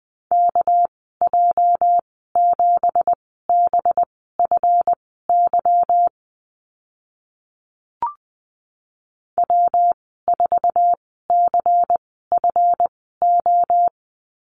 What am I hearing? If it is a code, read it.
Morse code, heard as a keyed tone, about 19 wpm, spaced as for 14 wpm, 700 Hz. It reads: KJ7BFY W4CFO
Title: 213 Lesson 34g - Number 8 - Call Signs 20wpm